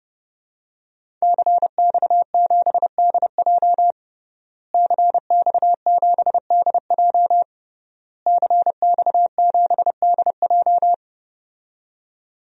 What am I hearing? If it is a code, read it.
Morse code, heard as a keyed tone, about 32 wpm, 700 Hz. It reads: CX7DJ CX7DJ CX7DJ
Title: DXCC-100 Call Signs - 5 Characters - Repeated 3x - 30wpm